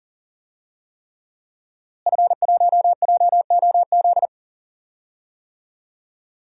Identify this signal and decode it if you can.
Morse code, heard as a keyed tone, about 38 wpm, 700 Hz. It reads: F1JOZ
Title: DXCC-100 Call Signs - 5 Characters - Repeated 3x - 40wpm